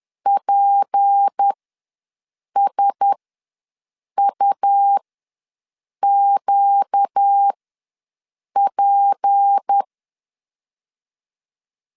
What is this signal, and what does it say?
Morse code, heard as a keyed tone, about 10 wpm, spaced as for 6 wpm, 800 Hz. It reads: PSUQP
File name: LEZIONI / 6 PKX0 / 6 P ws 10wpm 3751